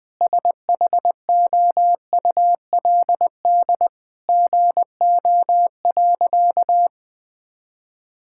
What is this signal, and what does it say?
Morse code, heard as a keyed tone, about 21 wpm, 700 Hz. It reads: SHOULD GO.
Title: Sentences with 3-Word Phrases - Spoken Beginning - 20wpm